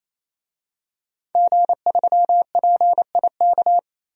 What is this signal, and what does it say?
Morse code, heard as a keyed tone, about 28 wpm, 700 Hz. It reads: G3PIK
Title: DXCC-100 Call Signs - 5 Characters - 28wpm